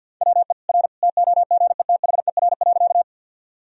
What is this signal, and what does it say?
Morse code, heard as a keyed tone, about 50 wpm, 700 Hz. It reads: WE R TOGETHER.
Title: Sentences from Top 300 Words and Abbreviations 50wpm